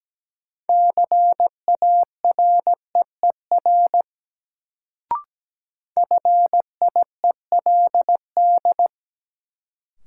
Morse code, heard as a keyed tone, about 17 wpm, 700 Hz. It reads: CAREER FIELD
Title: Sentences from Top 1000 Words - New Word Drill - 17wpm